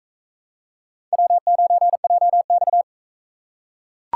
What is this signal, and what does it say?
Morse code, heard as a keyed tone, about 41 wpm, 700 Hz. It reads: W9JX